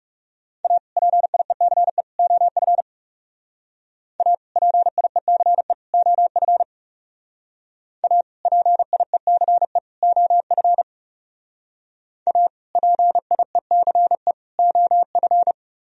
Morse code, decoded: A PIECE OF A PIECE OF A PIECE OF A PIECE OF

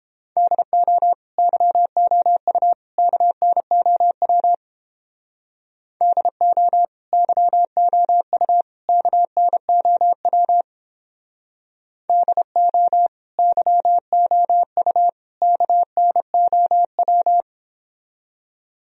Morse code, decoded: DO YOU KNOW DO YOU KNOW DO YOU KNOW

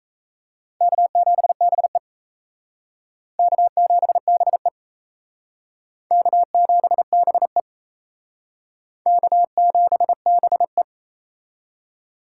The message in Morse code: K7BE K7BE K7BE K7BE

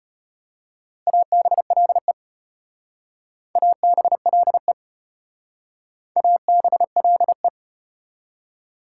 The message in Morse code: ABLE ABLE ABLE

